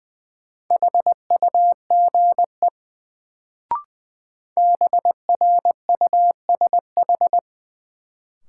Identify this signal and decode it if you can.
Morse code, heard as a keyed tone, about 20 wpm, 700 Hz. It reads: HUGE BRUSH
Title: Sentences from Top 1100 Words - New Word Drill Alt - 20wpm